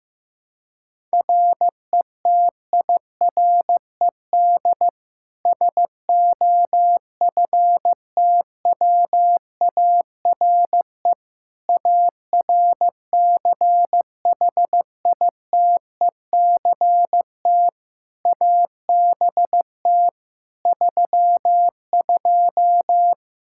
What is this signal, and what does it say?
Morse code, heard as a keyed tone, about 16 wpm, 700 Hz. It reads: RETIRED SOFTWARE ARCHITECT ABT 32
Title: Practice QSO - Exchange 3 - More Info - 15wpm